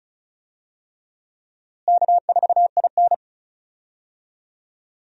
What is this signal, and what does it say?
Morse code, heard as a keyed tone, about 36 wpm, 700 Hz. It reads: K4IN
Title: USA Call Signs - 4 Characters - Repeated 3x - 35wpm